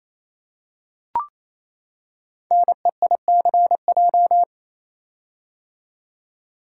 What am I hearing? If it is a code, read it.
Morse code, heard as a keyed tone, about 28 wpm, 700 Hz. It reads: NEICJ
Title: Code Groups - 5 Letters - 28wpm